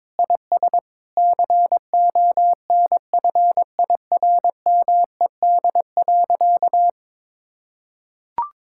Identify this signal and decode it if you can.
Morse code, heard as a keyed tone, about 22 wpm, 700 Hz. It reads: IS CONFIRMED.
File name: practice / 22wpm / Sentences with 8-Word Phrases - 22wpm